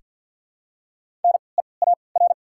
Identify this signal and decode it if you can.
Morse code, heard as a keyed tone, about 53 wpm, spaced as for 29 wpm, 700 Hz. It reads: NEAR
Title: 3x Character Spacing - Top 300 Words - 50wpm